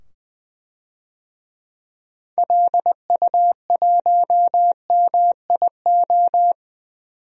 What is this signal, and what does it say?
Morse code, heard as a keyed tone, about 20 wpm, 700 Hz. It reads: LU1MIO
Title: DXCC-100 Call Signs - 6 Characters - 20wpm